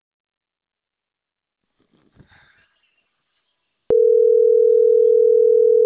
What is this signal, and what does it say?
Ringback tone (North America), 440 + 480 Hz.